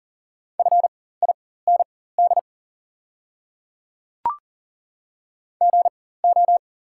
Morse code, heard as a keyed tone, about 39 wpm, spaced as for 19 wpm, 700 Hz. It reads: FIND GO
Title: 4x Character Spacing - Top 100 Words - 40wpm